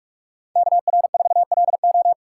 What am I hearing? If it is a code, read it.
Morse code, heard as a keyed tone, about 44 wpm, 700 Hz. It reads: KR4LO